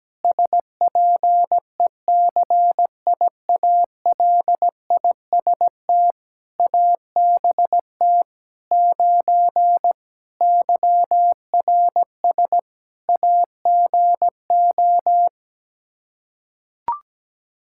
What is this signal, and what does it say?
Morse code, heard as a keyed tone, about 17 wpm, 700 Hz. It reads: SPECIALIST ABT 9 YRS AGO
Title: Practice QSO - Exchange 3 - More Info - 17wpm